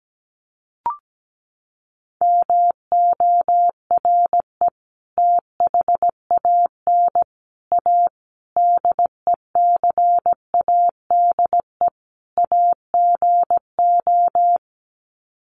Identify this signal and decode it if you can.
Morse code, heard as a keyed tone, about 17 wpm, 700 Hz. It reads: MORE THAN A DECADE AGO
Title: Sets of 5 Words Encore2 17wpm